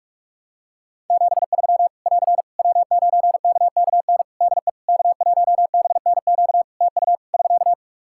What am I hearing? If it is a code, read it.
Morse code, heard as a keyed tone, about 48 wpm, 700 Hz. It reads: 73 <AR> W9KKN DE K1BNQ TU <SK>
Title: K1USN SST QSO Practice 45wpm